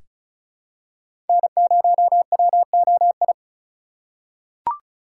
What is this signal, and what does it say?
Morse code, heard as a keyed tone, about 34 wpm, 700 Hz. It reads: N0WOI
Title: K1USN SST QSO Practice - 35wpm